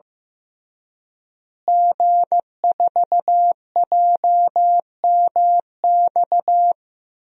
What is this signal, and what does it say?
Morse code, heard as a keyed tone, about 15 wpm, 700 Hz. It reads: G4JMX
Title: DXCC-100 Call Signs - 5 Characters - Repeated 3x - 15wpm